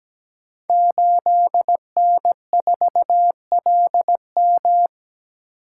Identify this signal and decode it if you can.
Morse code, heard as a keyed tone, about 17 wpm, 700 Hz. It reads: 8N4LM
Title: Code Groups - 5 Letter:Numbers - 17wpm